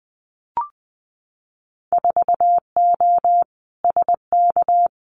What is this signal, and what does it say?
Morse code, heard as a keyed tone, about 19 wpm, 700 Hz. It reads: 4O SK
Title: Winter Field Day Exchanges Encore 20wpm